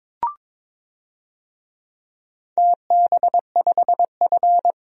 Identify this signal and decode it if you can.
Morse code, heard as a keyed tone, about 22 wpm, 700 Hz. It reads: TB5F
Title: DXCC-100 Call Signs - 4 Characters - Repeated 3x - 22wpm